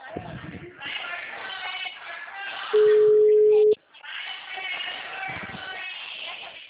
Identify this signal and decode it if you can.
Ringback tone (Europe, 425 Hz), with speech.
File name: out- -2048- - - .14